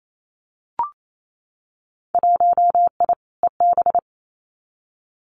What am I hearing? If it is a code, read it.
Morse code, heard as a keyed tone, about 28 wpm, 700 Hz. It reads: 1I EB
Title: Winter Field Day Exchanges Encore 28wpm